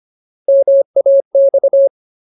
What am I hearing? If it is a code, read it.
Morse code, heard as a keyed tone, about 25 wpm, 550 Hz. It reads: MAX
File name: Ultimate Word List / 25WPM Ultimate